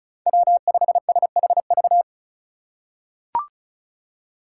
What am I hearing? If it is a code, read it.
Morse code, heard as a keyed tone, about 34 wpm, 700 Hz. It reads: W5SHV
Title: USA Call Signs - 5 Characters - 2x Character Spacing - 35wpm